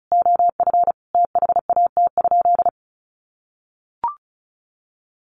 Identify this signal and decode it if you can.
Morse code, heard as a keyed tone, about 34 wpm, 700 Hz. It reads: OF THAT?
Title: Sentences from Top 1200 Words 35wpm